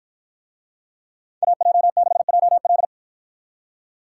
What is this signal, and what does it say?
Morse code, heard as a keyed tone, about 53 wpm, 700 Hz. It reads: AJ6JL